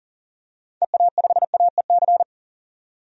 Morse code, heard as a keyed tone, about 40 wpm, 700 Hz. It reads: EA5AEC
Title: DXCC-100 Call Signs - 6 Characters - 40wpm